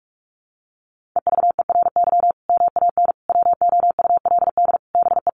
Morse code, heard as a keyed tone, about 44 wpm, 700 Hz. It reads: EVERY MAN WOULD BE
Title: Sentences from Top 1500 Words 45wpm